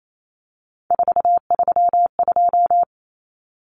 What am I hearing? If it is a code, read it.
Morse code, heard as a keyed tone, about 28 wpm, 700 Hz. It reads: 432
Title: Common Cut Numbers - 3-Digit 28wpm